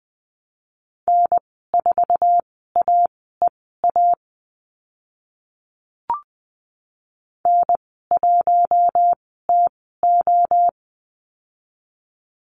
Morse code, heard as a keyed tone, about 20 wpm, spaced as for 14 wpm, 700 Hz. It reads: N4AEA N1TO
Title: 062 Lesson 11g - Letter h - Call Signs 20wpm